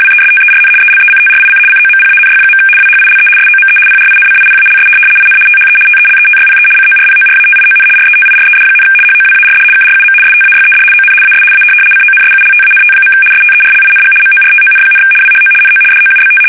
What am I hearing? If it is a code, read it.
RTTY, about 75 baud. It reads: WESTERN ELECTRIC COMPANY. WHILE SOME PRINCIPALS IN THE TELETYPE CORPORATION RETIRED, HOWARD KRUM STAYED ON AS A CONSULTANT. STERLING MORTON, WHO NO DOUBT
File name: fsk850-75B / teletype-corp